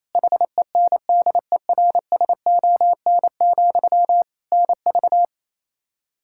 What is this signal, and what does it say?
Morse code, heard as a keyed tone, about 28 wpm, 700 Hz. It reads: HENDERSON, NV
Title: Practice QSO - Exchange 1 - Beginning - 28wpm